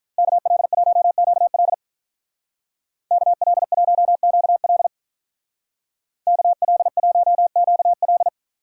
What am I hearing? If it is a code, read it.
Morse code, heard as a keyed tone, about 44 wpm, 700 Hz. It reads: KL1QL KL1QL KL1QL